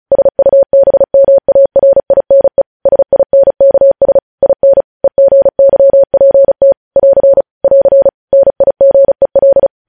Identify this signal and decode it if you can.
Morse code, heard as a keyed tone, about 36 wpm, 550 Hz. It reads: SUBMARINE SINKS IN EGYPT <AR> <AR> NIGEL
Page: News headlines in Morse code at 35 wpm